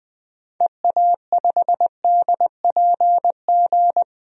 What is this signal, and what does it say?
Morse code, heard as a keyed tone, about 21 wpm, 700 Hz. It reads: EA5DPG
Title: DXCC-100 Call Signs - 6 Characters - 20wpm